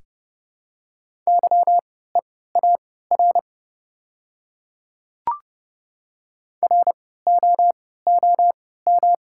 Morse code, heard as a keyed tone, about 32 wpm, spaced as for 17 wpm, 700 Hz. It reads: YEAR ROOM
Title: 3x Character Spacing - Top 300 Words - 30wpm